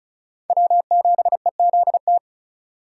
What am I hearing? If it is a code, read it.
Morse code, heard as a keyed tone, about 45 wpm, 700 Hz. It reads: W7EZT